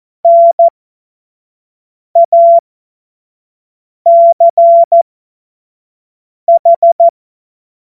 Morse code, heard as a keyed tone, about 14 wpm, spaced as for 5 wpm, 680 Hz. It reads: NACH
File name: Morsekurs / Witze / Witz 14. Februar 2023 25 F70